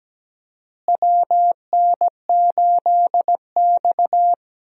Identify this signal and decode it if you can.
Morse code, heard as a keyed tone, about 17 wpm, 700 Hz. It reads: WN8X
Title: USA Call Signs - 4 Characters - Repeated 3x - 17wpm